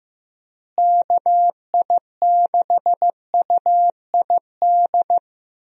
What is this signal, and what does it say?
Morse code, heard as a keyed tone, about 15 wpm, 700 Hz. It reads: KI6UID
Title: USA Call Signs - 6 Characters - Repeated 2x - 15wpm